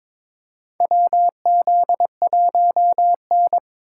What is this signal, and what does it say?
Morse code, heard as a keyed tone, about 21 wpm, 700 Hz. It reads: WZ1N
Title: USA Call Signs - 3x Character Spacing - 22wpm